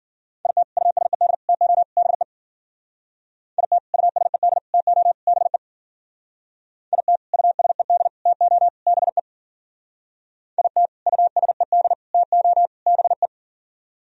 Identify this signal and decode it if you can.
Morse code, heard as a keyed tone, about 50 wpm, 700 Hz. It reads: IT USED TO BE IT USED TO BE IT USED TO BE IT USED TO BE
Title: Speed-Racing - Sets of 4 Words 40wpm